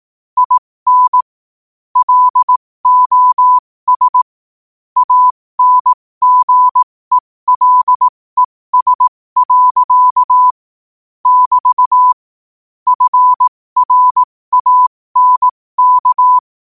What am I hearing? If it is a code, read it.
Morse code, heard as a keyed tone, about 18 wpm, spaced as for 15 wpm, 1000 Hz. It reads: IN LOS ANGELES. <BT> FRANK
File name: qotdfiles / 15wpm